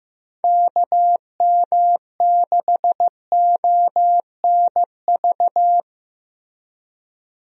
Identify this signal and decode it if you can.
Morse code, heard as a keyed tone, about 15 wpm, 700 Hz. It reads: KM6ONV
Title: USA Call Signs - 6 Characters - Repeated 3x - 15wpm